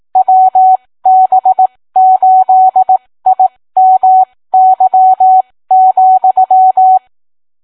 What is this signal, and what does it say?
Morse code, heard as a keyed tone, about 18 wpm, spaced as for 15 wpm, 750 Hz. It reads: WB8IMY,